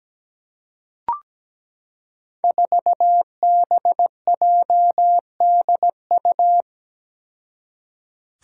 Morse code, heard as a keyed tone, about 17 wpm, 700 Hz. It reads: 4BJDU
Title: Five Challenging Letter-Number Combinations 17wpm